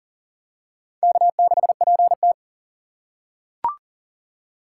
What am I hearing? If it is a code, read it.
Morse code, heard as a keyed tone, about 39 wpm, 690 Hz. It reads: K6PT
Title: USA Call Signs - 2x Character Spacing - 40wpm